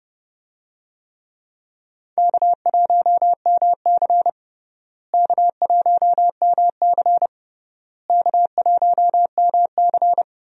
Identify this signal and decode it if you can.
Morse code, heard as a keyed tone, about 29 wpm, 700 Hz. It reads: K1MC K1MC K1MC
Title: USA Call Signs - 4 Characters - Repeated 3x - 30wpm